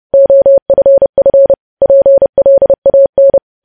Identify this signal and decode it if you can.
Morse code, heard as a keyed tone, about 30 wpm, 550 Hz. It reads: OFF PLAN
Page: news headlines in morse code at 30 wpm